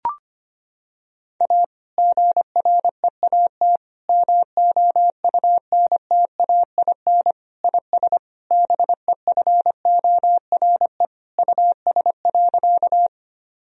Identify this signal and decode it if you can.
Morse code, heard as a keyed tone, about 25 wpm, 700 Hz. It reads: A GREAT MOUNTAIN IS BEFORE US.
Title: Sentences from Top 1000 Words - Review Part 1 - 25wpm